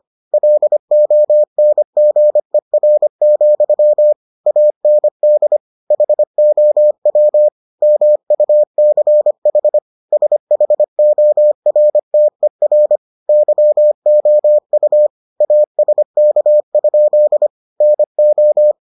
Morse code, heard as a keyed tone, about 25 wpm, 600 Hz. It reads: LONGER, AND HOW MUCH SHORTER YOU ASK? NO